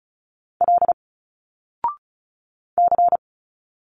Morse code, heard as a keyed tone, about 36 wpm, 700 Hz. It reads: L C
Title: Single Letters 35wpm